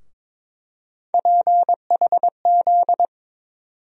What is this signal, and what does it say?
Morse code, heard as a keyed tone, about 22 wpm, 700 Hz. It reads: PHZ
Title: USA Call Sign Postfixes - 3 Letters 22wpm